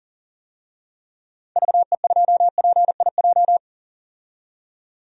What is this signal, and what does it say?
Morse code, heard as a keyed tone, about 40 wpm, 700 Hz. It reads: VE2PIJ
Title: DXCC-100 Call Signs - 6 Characters - 40wpm